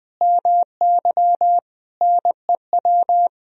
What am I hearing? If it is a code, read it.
Morse code, heard as a keyed tone, about 20 wpm, 700 Hz. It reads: MY NEW
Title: Sentences from Top 1000 Words - Review Part 4 - 20wpm